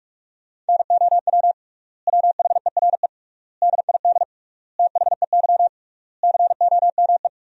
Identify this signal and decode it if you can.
Morse code, heard as a keyed tone, about 46 wpm, 700 Hz. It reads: NOW WHERE DID THEY COME
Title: 3x Word-Spacing - Sentences from Top 200 Words - 45wpm